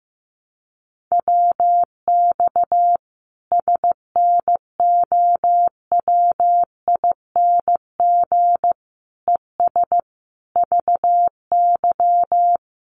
Morse code, heard as a keyed tone, about 15 wpm, 700 Hz. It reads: WX SNOWING ES VY
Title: Common QSO Phrases 15wpm